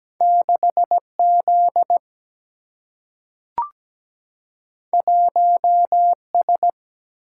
Morse code, heard as a keyed tone, about 17 wpm, 700 Hz. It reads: 6 Z 1 S
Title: Single-Letter Single-Number 17wpm